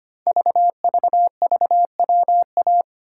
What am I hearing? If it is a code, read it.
Morse code, heard as a keyed tone, about 25 wpm, 700 Hz. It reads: VVVWA
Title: Code Groups - 5 Letters - 25wpm